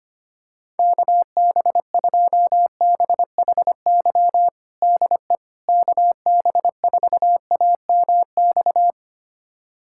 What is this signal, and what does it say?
Morse code, heard as a keyed tone, about 25 wpm, 700 Hz. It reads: KB2BHY DE KB4AMX
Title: USA Callsign DE Callsign - 6 Characters - 25wpm